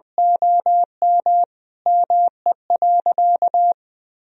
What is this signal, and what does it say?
Morse code, heard as a keyed tone, about 21 wpm, 700 Hz. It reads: OM ME.
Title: Sentences from Top 1000 Words - Review Part 2 - 20wpm